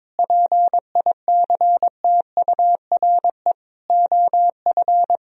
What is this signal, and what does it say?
Morse code, heard as a keyed tone, about 22 wpm, 700 Hz. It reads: PICTURE OF
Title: Sentences from Top 1000 Words - Review Part 1 - 22wpm